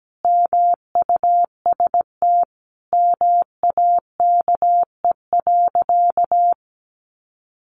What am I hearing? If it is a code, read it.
Morse code, heard as a keyed tone, about 17 wpm, 700 Hz. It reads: MUST MAKE.
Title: Sentences from Top 1000 Words 17wpm